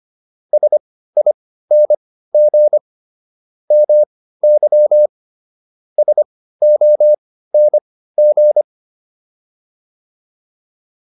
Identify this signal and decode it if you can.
Morse code, heard as a keyed tone, about 25 wpm, spaced as for 15 wpm, 600 Hz. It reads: SING MY SONG
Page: intermediate-practice-files